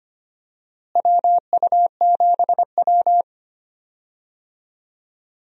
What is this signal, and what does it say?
Morse code, heard as a keyed tone, about 25 wpm, 700 Hz. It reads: WU7W
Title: USA Call Signs - 4 Characters - Repeated 3x - 25wpm